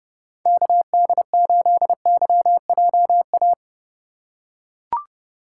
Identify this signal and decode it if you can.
Morse code, heard as a keyed tone, about 30 wpm, 700 Hz. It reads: KD8YJA
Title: USA Call Signs - 2x Character Spacing - 30wpm